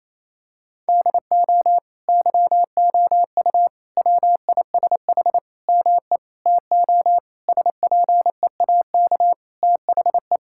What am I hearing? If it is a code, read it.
Morse code, heard as a keyed tone, about 27 wpm, 700 Hz. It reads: DO YOU WISH ME TO SPEAK THE
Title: Sentences from Top 1600 Words 28wpm